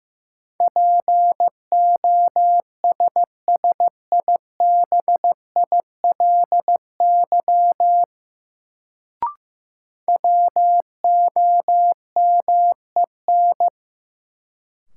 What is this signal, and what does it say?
Morse code, heard as a keyed tone, about 15 wpm, 700 Hz. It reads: POSSIBILY WOMEN
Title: Sentences from Top 1300 Words - New Word Drill - 15wpm